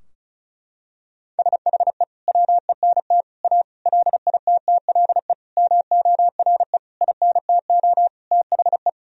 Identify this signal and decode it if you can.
Morse code, heard as a keyed tone, about 36 wpm, 700 Hz. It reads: SHE WENT A LITTLE MORE INTO THE
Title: Sentences from Top 1000 Words - Review Part 2 - 35wpm